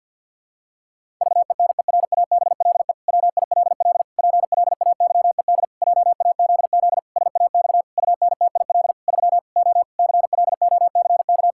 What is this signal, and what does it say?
Morse code, heard as a keyed tone, about 50 wpm, 700 Hz. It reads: VENERABLE WILL PLAYED JAZZ SAX UNTIL 3 O CLOCK
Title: Pangram Challenge v2 - 50wpm